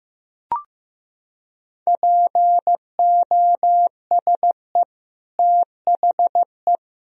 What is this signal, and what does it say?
Morse code, heard as a keyed tone, about 15 wpm, 700 Hz. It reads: POSE THE
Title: Sentences from Top 1000 Words - Review Part 2 - 15wpm